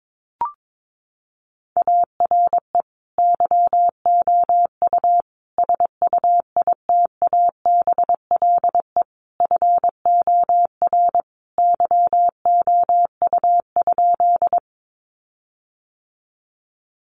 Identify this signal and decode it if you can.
Morse code, heard as a keyed tone, about 22 wpm, 700 Hz. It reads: ARE YOU SUITABLE FOR YOU?
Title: Sentences from Top 1400 Words 22wpm